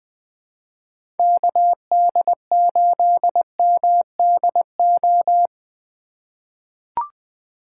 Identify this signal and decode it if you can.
Morse code, heard as a keyed tone, about 20 wpm, 700 Hz. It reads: KD8MDO
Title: USA Call Signs - 4x Character Spacing - 20wpm